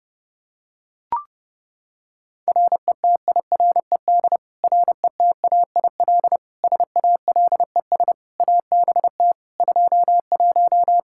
Morse code, heard as a keyed tone, about 32 wpm, 700 Hz. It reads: RETIRED RETAIL SALES ABT 21
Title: Practice QSO - Exchange 3 - More Info - 30wpm